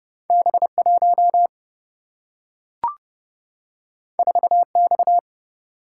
Morse code, heard as a keyed tone, about 30 wpm, 700 Hz. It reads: B1 4X
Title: Single-Letter Single-Number 30wpm